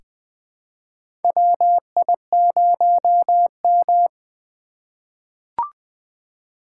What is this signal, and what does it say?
Morse code, heard as a keyed tone, about 21 wpm, 700 Hz. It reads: WI0M
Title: USA Call Signs - 4 Characters - 6x Character Spacing - 20wpm